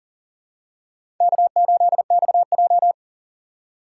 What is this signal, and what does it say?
Morse code, heard as a keyed tone, about 40 wpm, 700 Hz. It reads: K8XJ
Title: K1USN SST QSO Practice - 40wpm